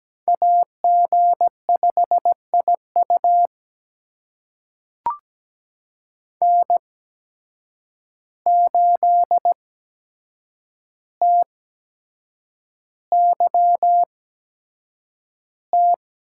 Morse code, heard as a keyed tone, about 17 wpm, spaced as for 7 wpm, 700 Hz. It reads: AG5IU N8TYT
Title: USA Call Signs - 5 Characters - 8x Character Spacing - 17wpm